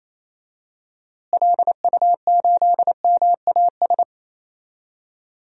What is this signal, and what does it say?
Morse code, heard as a keyed tone, about 28 wpm, 700 Hz. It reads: LU8MAS
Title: DXCC-100 Call Signs - 6 Characters - Repeated 2x - 28wpm